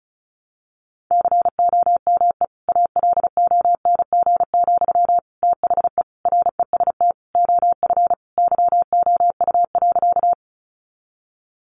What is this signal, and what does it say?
Morse code, heard as a keyed tone, about 34 wpm, 700 Hz. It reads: COME ALONG, THE REST OF YOU.
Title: Sentences from Top 700 Words 35wpm